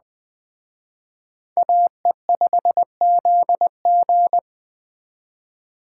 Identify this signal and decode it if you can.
Morse code, heard as a keyed tone, about 21 wpm, 700 Hz. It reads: AE5ZG
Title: Speed Racing - USA 2x2 Call Signs 17wpm